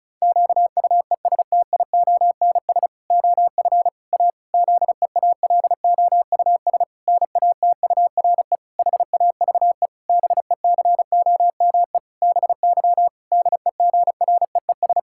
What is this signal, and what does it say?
Morse code, heard as a keyed tone, about 34 wpm, 700 Hz. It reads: QUESTIONS OF A ZEALOUS NATURE HAVE BECOME BY DEGREES
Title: Pangram Challenge v2 - 35wpm